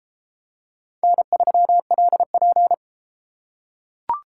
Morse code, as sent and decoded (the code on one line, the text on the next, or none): -. ...-- .-.. .--.
N3LP